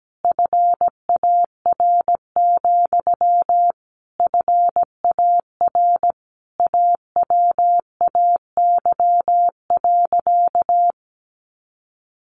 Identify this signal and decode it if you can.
Morse code, heard as a keyed tone, about 17 wpm, 700 Hz. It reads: FAR, FAR AWAY.
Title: Sentences from Top 600 Words 17wpm